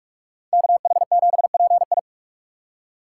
Morse code, heard as a keyed tone, about 58 wpm, 700 Hz. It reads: KH7PI